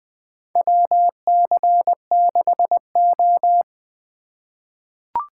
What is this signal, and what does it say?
Morse code, heard as a keyed tone, about 20 wpm, 700 Hz. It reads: WC6O
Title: USA Call Signs - 4 Characters - 4x Character Spacing - 20wpm